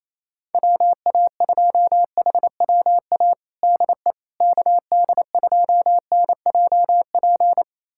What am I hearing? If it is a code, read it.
Morse code, heard as a keyed tone, about 27 wpm, 700 Hz. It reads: WA2HWA DE KD2NJP
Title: USA Callsign DE Callsign - 6 Characters - 28wpm